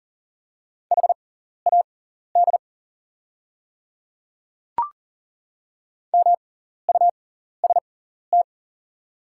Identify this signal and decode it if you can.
Morse code, heard as a keyed tone, about 40 wpm, spaced as for 14 wpm, 700 Hz. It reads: HAD MUST
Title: 6x Character Spacing - Top 300 Words - 40wpm